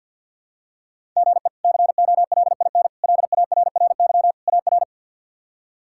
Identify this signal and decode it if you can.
Morse code, heard as a keyed tone, about 51 wpm, 700 Hz. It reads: GE COLIN LARRY AR
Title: K1USN SST QSO Practice - 50wpm